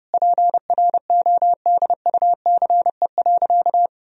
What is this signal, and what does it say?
Morse code, heard as a keyed tone, about 32 wpm, 700 Hz. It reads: PRODUCE.
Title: Sentences from Top 1000 Words - Review Part 3 - 30wpm